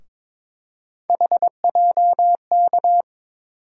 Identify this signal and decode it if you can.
Morse code, heard as a keyed tone, about 22 wpm, 700 Hz. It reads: HJK
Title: USA Call Sign Postfixes - 3 Letters 22wpm